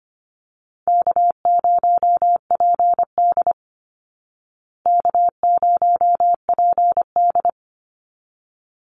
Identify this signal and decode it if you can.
Morse code, heard as a keyed tone, about 25 wpm, 700 Hz. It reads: K0PD K0PD
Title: Easy Call Signs Repeated 2x - 25wpm